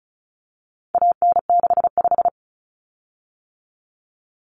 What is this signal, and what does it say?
Morse code, heard as a keyed tone, about 34 wpm, 700 Hz. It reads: AN65
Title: Common Cut Numbers - 4-Digit 35wpm